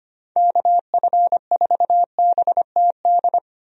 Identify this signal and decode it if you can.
Morse code, heard as a keyed tone, about 25 wpm, 700 Hz. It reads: KF4BTD
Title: USA Call Signs - 6 Characters - 25wpm